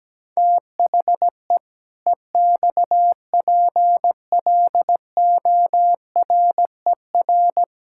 Morse code, heard as a keyed tone, about 17 wpm, 700 Hz. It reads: THE EXPLORER
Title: Pangram Challenge v2 - 17wpm